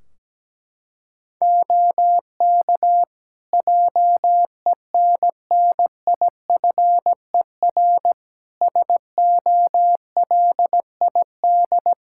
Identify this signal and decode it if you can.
Morse code, heard as a keyed tone, about 17 wpm, 700 Hz. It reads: OK JENNIFER SOLID
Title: Practice QSO - Exchange 3 - More Info - 17wpm